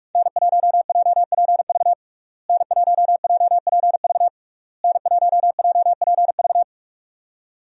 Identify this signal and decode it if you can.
Morse code, heard as a keyed tone, about 46 wpm, 700 Hz. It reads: N1JPV N1JPV N1JPV
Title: USA Call Signs - 5 Characters - Repeated 3x - 45wpm